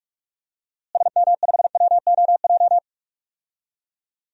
Morse code, heard as a keyed tone, about 46 wpm, 700 Hz. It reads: SM5WOJ
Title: DXCC-100 Call Signs - 6 Characters - Repeated 3x - 45wpm